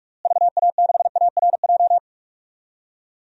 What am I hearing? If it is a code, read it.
Morse code, heard as a keyed tone, about 41 wpm, 700 Hz. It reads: VA6ARJ
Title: DXCC-100 Call Signs - 6 Characters - Repeated 2x - 45wpm